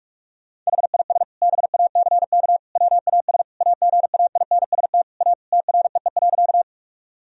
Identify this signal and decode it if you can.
Morse code, heard as a keyed tone, about 46 wpm, 700 Hz. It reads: HIS BACK WAS AGAINST A TREE.
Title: Sentences from Top 1000 Words - Review Part 1 - 45wpm